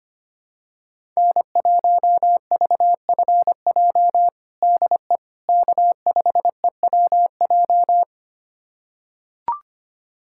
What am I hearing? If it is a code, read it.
Morse code, heard as a keyed tone, about 25 wpm, 700 Hz. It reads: N1VFJ DE K5EWJ
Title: USA Callsign DE Callsign - 5 Characters - 25wpm